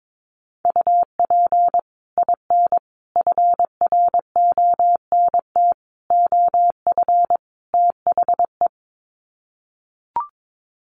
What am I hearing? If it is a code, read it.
Morse code, heard as a keyed tone, about 22 wpm, 700 Hz. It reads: UP IN FRONT OF THE
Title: Sets of 5 Words Encore2 22wpm